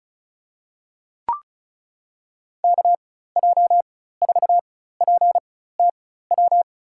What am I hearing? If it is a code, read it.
Morse code, heard as a keyed tone, about 36 wpm, spaced as for 16 wpm, 700 Hz. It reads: KJ4PTW
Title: USA Call Signs - 6 Characters - 4x Character Spacing - 35wpm